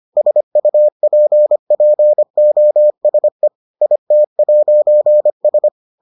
Morse code, heard as a keyed tone, about 25 wpm, 600 Hz. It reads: SUPPOSE IT'S